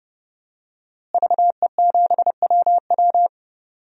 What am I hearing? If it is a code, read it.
Morse code, heard as a keyed tone, about 28 wpm, 700 Hz. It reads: VE7WW
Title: DXCC-100 Call Signs - 5 Characters - Repeated 3x - 30wpm